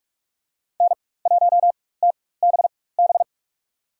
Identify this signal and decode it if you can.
Morse code, heard as a keyed tone, about 46 wpm, spaced as for 21 wpm, 700 Hz. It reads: N1TBB
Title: USA Call Signs - 5 Characters - 4x Character Spacing - 45wpm